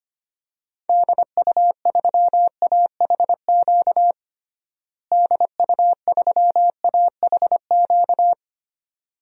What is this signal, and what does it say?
Morse code, heard as a keyed tone, about 25 wpm, 700 Hz. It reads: DU3AHQ DU3AHQ
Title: DXCC-100 Call Signs - 6 Characters - Repeated 2x - 25wpm